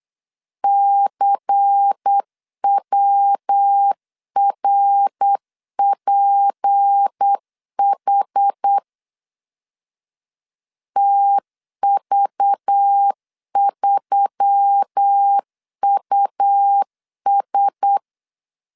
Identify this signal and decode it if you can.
Morse code, heard as a keyed tone, about 8 wpm, 800 Hz. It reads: CWRPH TV3US
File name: LEZIONI / 6 PKX0 / 6 PK 8wpm 3003